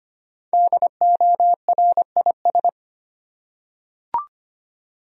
Morse code, heard as a keyed tone, about 25 wpm, 700 Hz. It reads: DORIS
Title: USA Common Names - 25wpm